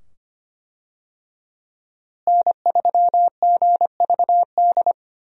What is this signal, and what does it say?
Morse code, heard as a keyed tone, about 25 wpm, 700 Hz. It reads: N3GVD
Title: USA Call Signs - 5 Characters - Repeated 2x - 25wpm